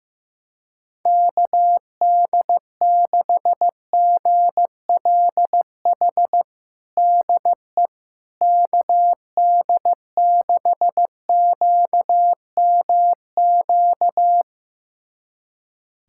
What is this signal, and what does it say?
Morse code, heard as a keyed tone, about 15 wpm, 700 Hz. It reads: KD6GLH DE KD6QMQ
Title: USA Callsign DE Callsign - 6 Characters - 15wpm